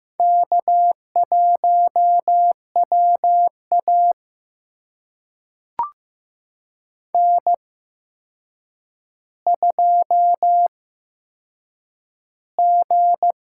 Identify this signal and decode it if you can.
Morse code, heard as a keyed tone, about 15 wpm, spaced as for 7 wpm, 700 Hz. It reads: K1WA N2G
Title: USA Call Signs - 4 Characters - 8x Character Spacing - 15wpm